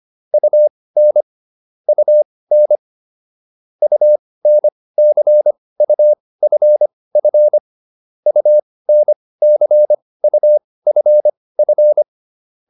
Morse code, heard as a keyed tone, about 25 wpm, spaced as for 18 wpm, 600 Hz. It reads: UN UN UNCUFF UNCUFF